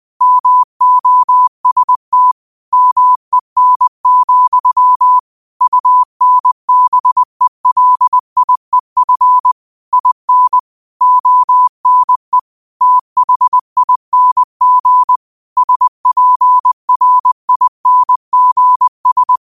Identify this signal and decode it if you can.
Morse code, heard as a keyed tone, about 20 wpm, 1000 Hz. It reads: MOST MEN, UNBELIEF IN ONE THING SPRINGS
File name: qotdfiles / 20wpm